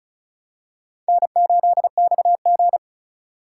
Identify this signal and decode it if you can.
Morse code, heard as a keyed tone, about 34 wpm, 700 Hz. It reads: N8XG
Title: USA Call Signs - 4 Characters - 3x Character Spacing - 35wpm